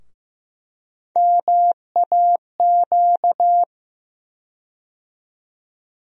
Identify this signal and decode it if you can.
Morse code, heard as a keyed tone, about 15 wpm, 700 Hz. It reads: MAQ
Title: USA Call Sign Postfixes - 3 Letters 15wpm